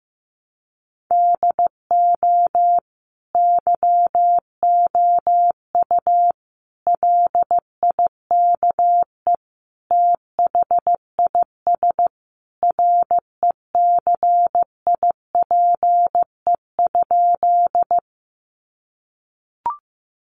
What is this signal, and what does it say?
Morse code, heard as a keyed tone, about 15 wpm, 700 Hz. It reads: DO YOU LIKE THIS RECIPE?
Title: Sentences from Top 1200 Words 15wpm